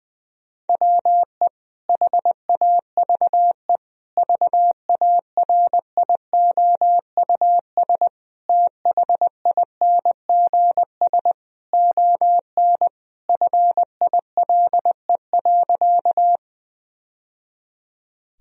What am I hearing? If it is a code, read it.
Morse code, heard as a keyed tone, about 20 wpm, 700 Hz. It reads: WE HAVE VARIOUS THINGS ON FILE.
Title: Sentences from Top 1000 Words - Review Part 4 - 20wpm